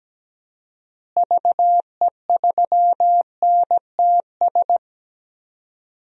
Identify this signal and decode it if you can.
Morse code, heard as a keyed tone, about 17 wpm, 700 Hz. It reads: VE3NTS
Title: DXCC-100 Call Signs - 6 Characters - Repeated 3x - 17wpm